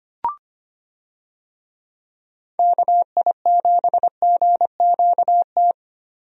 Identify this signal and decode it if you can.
Morse code, heard as a keyed tone, about 25 wpm, 700 Hz. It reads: KI7GQT
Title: USA Call Signs - 6 Characters - 25wpm